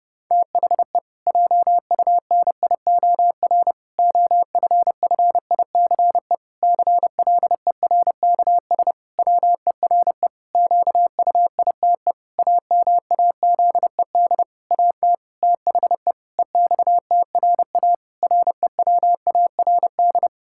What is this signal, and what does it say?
Morse code, heard as a keyed tone, about 30 wpm, 700 Hz. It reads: THE JUNIOR OFFICE CLERKS WERE QUITE AMAZED AT THE EXTRA REWARD
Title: Pangram Challenge v2 - 30wpm